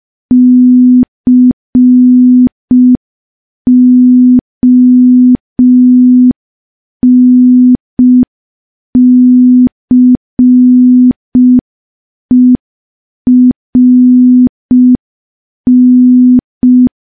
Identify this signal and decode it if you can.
Morse code, heard as a keyed tone, about 5 wpm, 250 Hz. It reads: CONCERN
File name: morse / morse-training-05WPM